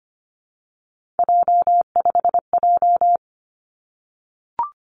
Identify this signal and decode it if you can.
Morse code, heard as a keyed tone, about 25 wpm, 700 Hz. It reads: J5J
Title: Three Challenging Letter-Number Combinations 25wpm